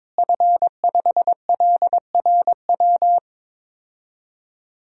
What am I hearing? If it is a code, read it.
Morse code, heard as a keyed tone, about 22 wpm, 700 Hz. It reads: F5LRW
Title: DXCC-100 Call Signs - 5 Characters - Repeated 2x - 22wpm